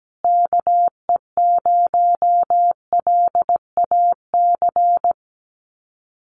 Morse code, heard as a keyed tone, about 17 wpm, 700 Hz. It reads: KE0LAC
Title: USA Call Signs Repeated 3x 17wpm